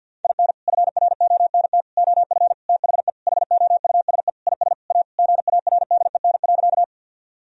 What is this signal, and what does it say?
Morse code, heard as a keyed tone, about 50 wpm, 700 Hz. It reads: IN FRONT OF THE HOUSE IS A GARDEN.
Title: Sentences from Top 1000 Words - Review Part 4 - 50wpm